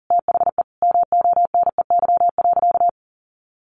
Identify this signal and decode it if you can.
Morse code, heard as a keyed tone, about 40 wpm, 700 Hz. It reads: THE MONEY.
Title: Sentences from Top 1400 Words 40wpm